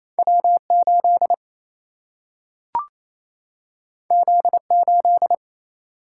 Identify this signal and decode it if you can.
Morse code, heard as a keyed tone, about 28 wpm, 700 Hz. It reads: W 8 Z 8
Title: Single-Letter Single-Number 28wpm